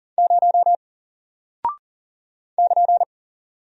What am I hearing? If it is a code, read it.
Morse code, heard as a keyed tone, about 40 wpm, 700 Hz. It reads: 0 <KN>
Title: Single Character (Letter:Number:Punctuation:Prosign) Alt 40wpm